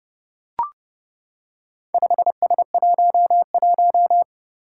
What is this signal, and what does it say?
Morse code, heard as a keyed tone, about 30 wpm, 700 Hz. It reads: 5S11
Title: Four Challenging Letter-Number Combinations 30wpm